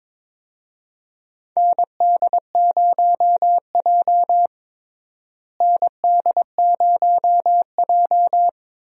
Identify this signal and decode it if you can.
Morse code, heard as a keyed tone, about 22 wpm, 700 Hz. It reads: ND0J ND0J
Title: USA Call Signs - 4 Characters - Repeated 2x - 22wpm